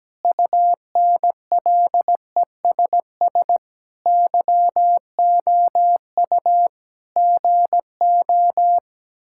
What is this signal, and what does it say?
Morse code, heard as a keyed tone, about 17 wpm, 700 Hz. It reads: UNLESS YOU GO
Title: Sentences from Top 1000 Words - Review Part 6 - 17wpm